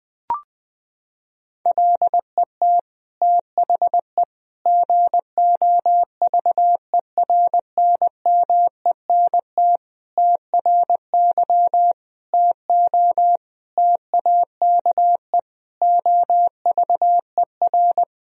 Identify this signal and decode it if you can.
Morse code, heard as a keyed tone, about 20 wpm, 700 Hz. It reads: LET THE GOVERNMENT TRY TO TAKE OVER
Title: Sentences from Top 1000 Words - Review Part 3 - 20wpm